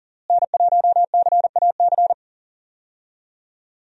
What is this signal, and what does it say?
Morse code, heard as a keyed tone, about 37 wpm, 700 Hz. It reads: N1CAC
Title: USA Call Signs - 5 Characters - Repeated 2x - 40wpm